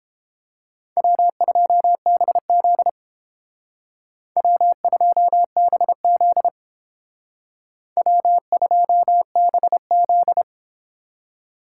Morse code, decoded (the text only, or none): W2BZ W2BZ W2BZ